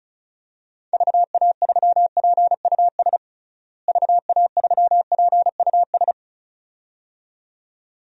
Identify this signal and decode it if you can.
Morse code, heard as a keyed tone, about 34 wpm, 700 Hz. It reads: VA3PUS VA3PUS
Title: DXCC-100 Call Signs - 6 Characters - Repeated 2x - 35wpm